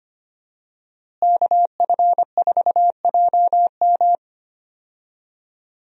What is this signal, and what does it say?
Morse code, heard as a keyed tone, about 25 wpm, 700 Hz. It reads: KF4JM
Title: USA Call Signs - 5 Characters - Repeated 3x - 25wpm